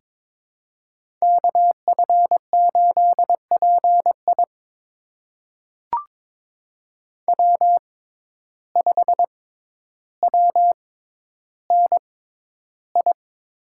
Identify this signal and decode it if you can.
Morse code, heard as a keyed tone, about 22 wpm, spaced as for 11 wpm, 700 Hz. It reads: KF8PI W5WNI
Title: USA Call Signs - 5 Characters - 6x Character Spacing - 22wpm